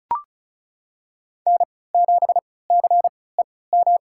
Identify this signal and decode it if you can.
Morse code, heard as a keyed tone, about 36 wpm, spaced as for 20 wpm, 700 Hz. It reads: N7CEM
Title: USA Call Signs - 5 Characters - 3x Character Spacing - 35wpm